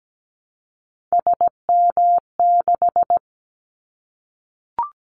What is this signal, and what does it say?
Morse code, heard as a keyed tone, about 17 wpm, 700 Hz. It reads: SM6
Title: Callsign Prefixes for Top 25 DXCC - 17wpm